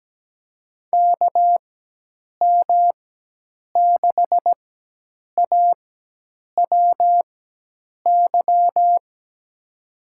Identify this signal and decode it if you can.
Morse code, heard as a keyed tone, about 17 wpm, spaced as for 8 wpm, 700 Hz. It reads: KM6AWY
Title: USA Call Signs - 6 Characters - 4x Character Spacing - 17wpm